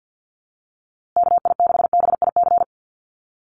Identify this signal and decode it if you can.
Morse code, heard as a keyed tone, about 49 wpm, 700 Hz. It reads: KI6BIC
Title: USA Call Signs Repeated 3x 50wpm